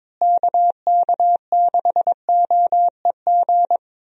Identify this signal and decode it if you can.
Morse code, heard as a keyed tone, about 22 wpm, 700 Hz. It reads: KK6OEG
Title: USA Call Signs - 6 Characters - Repeated 3x - 22wpm